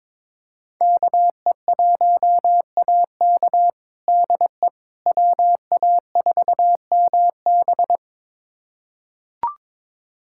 Morse code, heard as a keyed tone, about 21 wpm, 700 Hz. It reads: KE1AK DE WA4MB
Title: USA Callsign DE Callsign - 5 Characters - 22wpm